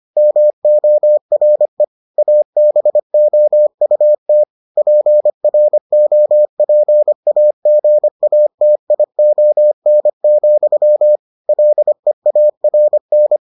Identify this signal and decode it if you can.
Morse code, heard as a keyed tone, about 25 wpm, 600 Hz. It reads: MORE ABOUT PROPAGATION, LEARN